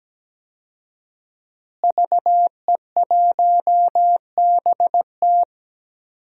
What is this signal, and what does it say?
Morse code, heard as a keyed tone, about 17 wpm, 700 Hz. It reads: VE1BT
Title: DXCC-100 Call Signs - 5 Characters - Repeated 2x - 17wpm